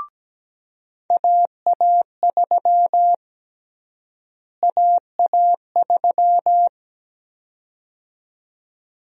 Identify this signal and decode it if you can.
Morse code, heard as a keyed tone, about 17 wpm, 700 Hz. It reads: AA3 AA3
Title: USA Call Sign Prefixes - Repeated 2x - 17wpm